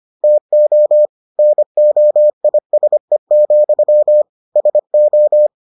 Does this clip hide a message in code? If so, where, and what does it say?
Yes: Morse code from 0.2 to 5.5 s, heard as a keyed tone, about 25 wpm, 600 Hz. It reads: TO NOISE, SO